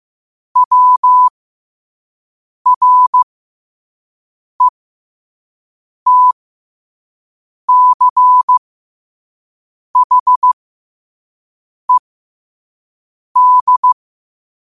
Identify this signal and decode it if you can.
Morse code, heard as a keyed tone, about 15 wpm, spaced as for 5 wpm, 1000 Hz. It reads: WRETCHED
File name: qotdfiles / 5wpm